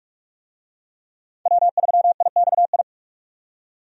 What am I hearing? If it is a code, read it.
Morse code, heard as a keyed tone, about 46 wpm, 700 Hz. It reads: W3IXI
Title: USA Call Signs - 5 Characters - 45wpm